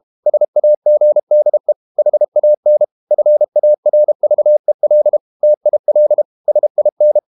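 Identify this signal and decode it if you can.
Morse code, heard as a keyed tone, about 32 wpm, 600 Hz. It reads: SAGDE HAN FARVEL TIL SIN
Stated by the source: High-speed QRQ CW at 32 WPM.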